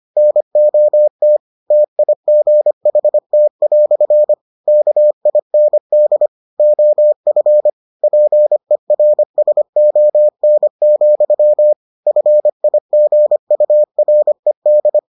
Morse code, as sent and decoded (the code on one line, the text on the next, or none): -. --- - / - .. --. .... - .-..-. / -.- .. -. -.. / --- ..-. / .--. .-. ... --- -. --..-- / ..-. .. --. ..- .-. -..
NOT TIGHT" KIND OF PERSON, FIGURED